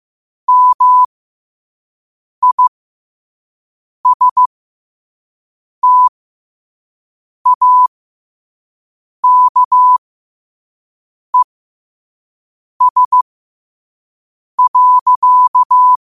Morse code, heard as a keyed tone, about 15 wpm, spaced as for 5 wpm, 1000 Hz. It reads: MISTAKES.
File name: 5wpm